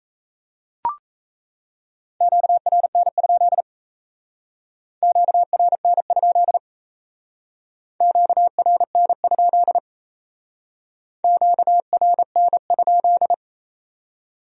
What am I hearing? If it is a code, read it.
Morse code, heard as a keyed tone, about 35 wpm, 700 Hz. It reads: QRN? QRN? QRN? QRN?